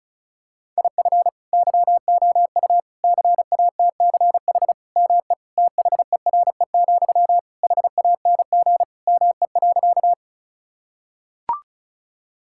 Morse code, heard as a keyed tone, about 36 wpm, 700 Hz. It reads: IF YOU CATCH ME THERE, HANG ME.
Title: Sentences from Top 1600 Words 35wpm